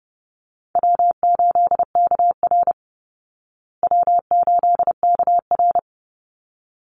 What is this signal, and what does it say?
Morse code, heard as a keyed tone, about 30 wpm, 700 Hz. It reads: W8KR W8KR
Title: Easy Call Signs Repeated 2x - 30wpm